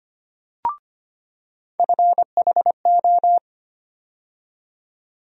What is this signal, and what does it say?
Morse code, heard as a keyed tone, about 25 wpm, 700 Hz. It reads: FHO
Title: USA Call Sign Postfixes - 3 Letters 25wpm